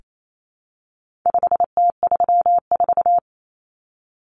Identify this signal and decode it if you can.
Morse code, heard as a keyed tone, about 28 wpm, 700 Hz. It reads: 5T34
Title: Common Cut Numbers - 4-Digit 28wpm